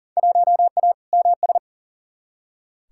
Morse code, heard as a keyed tone, about 38 wpm, 700 Hz. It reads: 1A MS
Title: Field Day Exchanges - 40wpm